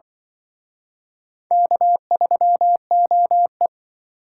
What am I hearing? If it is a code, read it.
Morse code, heard as a keyed tone, about 25 wpm, 700 Hz. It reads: K3OE
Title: Speed Racing - USA Call Signs 20wpm